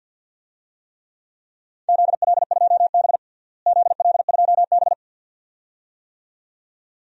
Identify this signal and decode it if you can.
Morse code, heard as a keyed tone, about 52 wpm, 700 Hz. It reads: ZL2B ZL2B
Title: DXCC-100 Call Signs - 4 Characters - Repeated 2x - 50wpm